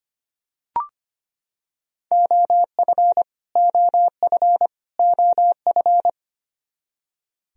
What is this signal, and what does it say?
Morse code, heard as a keyed tone, about 25 wpm, 700 Hz. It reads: OF OF OF
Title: Top 100 Words Repeated 25wpm